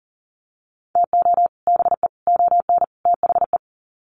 Morse code, heard as a keyed tone, about 40 wpm, 700 Hz. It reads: TO BE ON THE
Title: Sets of 4 Words 40wpm